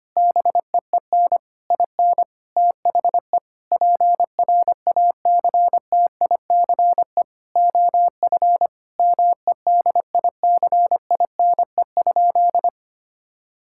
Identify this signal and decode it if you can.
Morse code, heard as a keyed tone, about 25 wpm, 700 Hz. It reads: BEEN IN THE PRACTICE OF MEDICINE?
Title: Sentences from Top 1000 Words - Review Part 6 - 25wpm